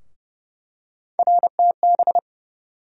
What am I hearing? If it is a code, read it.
Morse code, heard as a keyed tone, about 30 wpm, 700 Hz. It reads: RTB
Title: USA Call Sign Postfixes - 3 Letters 30wpm